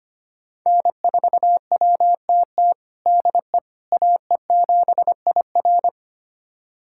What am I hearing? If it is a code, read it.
Morse code, heard as a keyed tone, about 25 wpm, 700 Hz. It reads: N4WTT DE AE7IR
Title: USA Callsign DE Callsign - 5 Characters - 25wpm